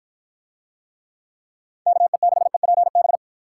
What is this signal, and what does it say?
Morse code, heard as a keyed tone, about 54 wpm, 700 Hz. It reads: KE6EPB